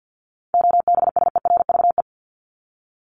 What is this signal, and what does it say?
Morse code, heard as a keyed tone, about 50 wpm, 700 Hz. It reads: OBSERVE
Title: ED Suffix - Top 100 Words with Base Words 50wpm